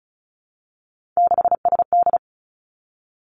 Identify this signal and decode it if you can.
Morse code, heard as a keyed tone, about 36 wpm, 700 Hz. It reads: 6SD
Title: Three Challenging Letter-Number Combinations 35wpm